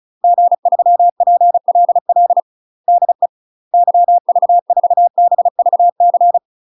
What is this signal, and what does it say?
Morse code, heard as a keyed tone, about 34 wpm, 700 Hz. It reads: G3PLL DE YV4BVC
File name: UK MASTER QSOs TRAINING / 35WPM / UK MASTER QSOs 0037